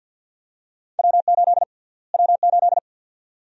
Morse code, heard as a keyed tone, about 52 wpm, 700 Hz. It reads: W8 W8
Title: USA Call Sign Prefixes - Repeated 2x - 50wpm